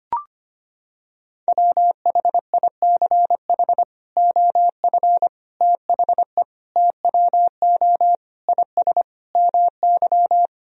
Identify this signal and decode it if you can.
Morse code, heard as a keyed tone, about 25 wpm, 700 Hz. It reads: WHICH OF THE TWO IS MY
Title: Sentences from Top 1000 Words - Review Part 2 - 25wpm